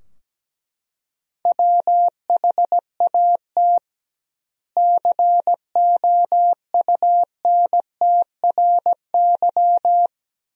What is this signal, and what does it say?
Morse code, heard as a keyed tone, about 17 wpm, 700 Hz. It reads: WHAT COUNTRY
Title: 2x Word-Spacing - Sentences from Top 300 Words - 17wpm